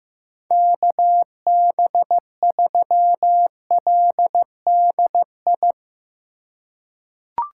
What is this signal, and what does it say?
Morse code, heard as a keyed tone, about 15 wpm, 700 Hz. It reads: KB3LDI
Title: USA Call Signs - 6 Characters - 3x Character Spacing - 15wpm